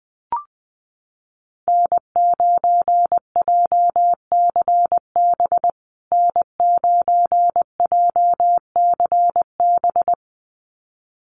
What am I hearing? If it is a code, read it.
Morse code, heard as a keyed tone, about 20 wpm, 700 Hz. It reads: N9JCB N9JCB
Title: USA Call Signs Repeated 2x 20wpm